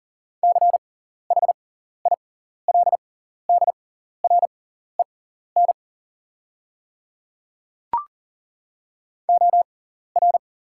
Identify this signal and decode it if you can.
Morse code, heard as a keyed tone, about 40 wpm, spaced as for 14 wpm, 700 Hz. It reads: CHILDREN OR
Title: 6x Character Spacing - Top 500 Words - 40wpm